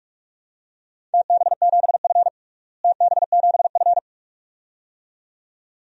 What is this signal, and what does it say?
Morse code, heard as a keyed tone, about 46 wpm, 700 Hz. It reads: TB7F TB7F
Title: DXCC-100 Call Signs - 4 Characters - Repeated 2x - 45wpm